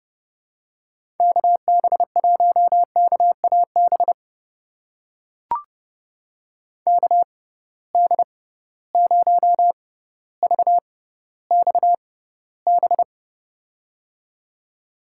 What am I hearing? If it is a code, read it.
Morse code, heard as a keyed tone, about 30 wpm, spaced as for 15 wpm, 700 Hz. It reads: KB1KAB KD0VXB
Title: USA Call Signs - 6 Characters - 6x Character Spacing - 30wpm